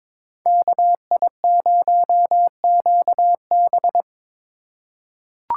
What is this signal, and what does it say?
Morse code, heard as a keyed tone, about 22 wpm, 700 Hz. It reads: KI0QB
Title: USA Call Signs - 5 Characters - 3x Character Spacing - 22wpm